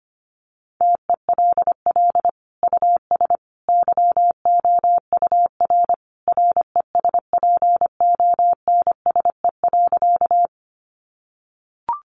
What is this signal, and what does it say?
Morse code, heard as a keyed tone, about 25 wpm, 700 Hz. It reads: TELL US YOUR RESPONSE.
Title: Sentences from Top 1400 Words 25wpm